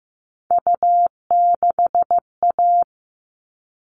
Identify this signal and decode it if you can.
Morse code, heard as a keyed tone, about 15 wpm, 700 Hz. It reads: U6A
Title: All Cut Numbers - 3-Digit 15wpm